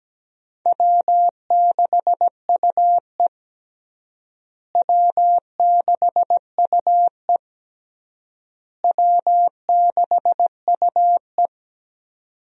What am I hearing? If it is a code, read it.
Morse code, heard as a keyed tone, about 17 wpm, 700 Hz. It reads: W6UE W6UE W6UE
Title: USA Call Signs - 4 Characters - Repeated 3x - 17wpm